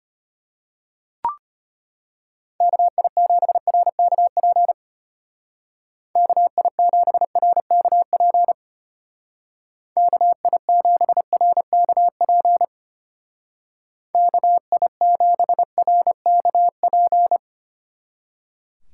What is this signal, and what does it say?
Morse code, heard as a keyed tone, about 31 wpm, 700 Hz. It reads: KI7RKP KI7RKP KI7RKP KI7RKP
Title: Speed Racing - USA 2x3 Call Signs 25wpm